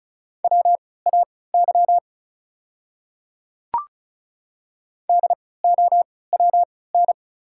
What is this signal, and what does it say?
Morse code, heard as a keyed tone, about 36 wpm, spaced as for 20 wpm, 690 Hz. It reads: WAY DOWN
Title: 3x Character Spacing - Top 100 Words - 35wpm